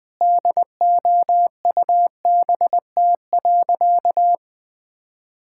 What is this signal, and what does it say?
Morse code, heard as a keyed tone, about 20 wpm, 700 Hz. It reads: DOUBT.
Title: Sentences with 7-Word Phrases - 20wpm